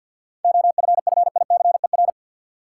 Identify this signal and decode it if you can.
Morse code, heard as a keyed tone, about 53 wpm, 700 Hz. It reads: OFFICER
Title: Seven Letter Words - 50wpm